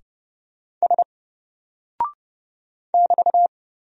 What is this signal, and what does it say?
Morse code, heard as a keyed tone, about 30 wpm, 700 Hz. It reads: S <BT>
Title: Single Character (Letter:Number:Punctuation:Prosign) 30wpm